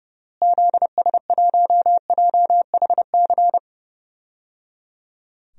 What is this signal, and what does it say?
Morse code, heard as a keyed tone, about 30 wpm, 700 Hz. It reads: ZS1JHC
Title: DXCC-100 Call Signs - 6 Characters - Repeated 3x - 30wpm